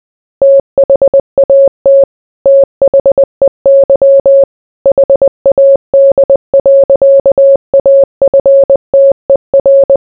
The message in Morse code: THAT THEY HAD. AFTER